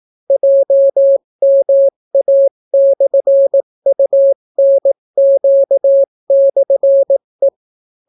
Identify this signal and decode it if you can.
Morse code, heard as a keyed tone, about 18 wpm, 550 Hz. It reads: JMA/UNQ/E